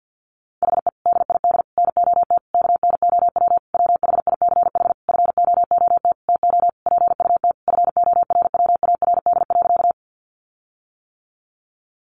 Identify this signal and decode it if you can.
Morse code, heard as a keyed tone, about 50 wpm, 700 Hz. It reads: HE DID NOT KNOW WHICH FOOT TO PUT FORWARD.
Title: Sentences from Top 1300 Words 50wpm